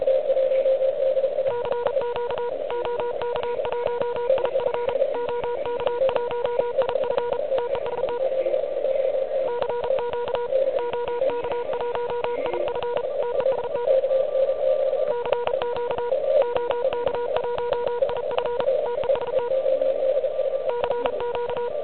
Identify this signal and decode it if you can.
Morse code, heard as a keyed tone, about 34 wpm, 960 Hz, with noise. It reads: CQ OK1IF OK1IF TEST CQ OK1IF TEST CQ OK1IF TEST CQ
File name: ARCHIV / QCX / QCX Pileup